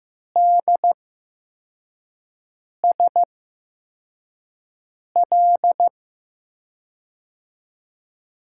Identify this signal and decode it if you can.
Morse code, heard as a keyed tone, about 15 wpm, spaced as for 4 wpm, 700 Hz. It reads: DSL
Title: USA Call Signs - 5 Characters - 8x Character Spacing - 15wpm